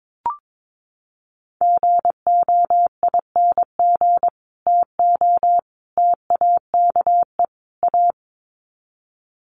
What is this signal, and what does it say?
Morse code, heard as a keyed tone, about 22 wpm, 700 Hz. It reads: GOING TO TAKE A
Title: Sets of 4 Words 22wpm